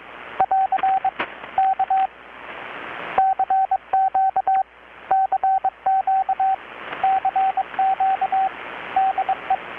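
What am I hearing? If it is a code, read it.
Morse code, heard as a keyed tone, about 22 wpm, 770 Hz, with radio noise. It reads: <AR> K CQ CQ CQ DE
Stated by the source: Morse from KPH's Night of Nights XX maritime radiotelegraphy event, received in Chiapas, Mexico about 3,800 km away on a Kenwood R-600 with a grounded 10 m long-wire antenna.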